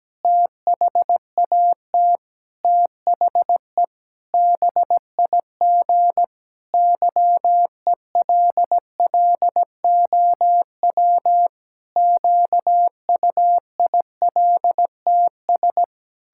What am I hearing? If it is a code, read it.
Morse code, heard as a keyed tone, about 17 wpm, 700 Hz. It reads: THAT THE BIG YELLOW QUILTS
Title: Pangram Challenge v2 - 17wpm